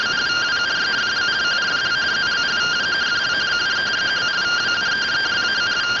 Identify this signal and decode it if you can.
Two-tone teleprinter signal, about 170 Hz shift.